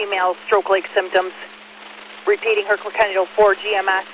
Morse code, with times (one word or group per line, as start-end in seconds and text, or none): none